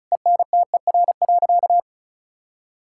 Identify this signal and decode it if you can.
Morse code, heard as a keyed tone, about 34 wpm, 700 Hz. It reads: ENTER.
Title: Sentences from Top 1600 Words 35wpm